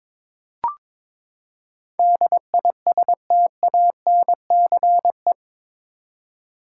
Morse code, heard as a keyed tone, about 22 wpm, 700 Hz. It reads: DISTANCE
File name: practice / 22wpm / Eight Letter Words - 22wpm